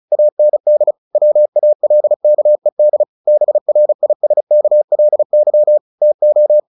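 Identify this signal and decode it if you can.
Morse code, heard as a keyed tone, about 36 wpm, 600 Hz. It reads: AND WALKED BRISKLY TO
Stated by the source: Morse practice at 35 wpm from CW med Gnister, sending Ken Follett's novel Never in English.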